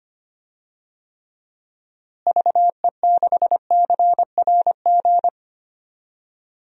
Morse code, heard as a keyed tone, about 25 wpm, 700 Hz. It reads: VE6CRG
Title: DXCC-100 Call Signs - 6 Characters - Repeated 3x - 25wpm